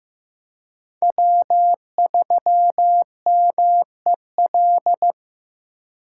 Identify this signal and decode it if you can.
Morse code, heard as a keyed tone, about 15 wpm, 700 Hz. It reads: W3MEL
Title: K1USN SST QSO Practice - 15wpm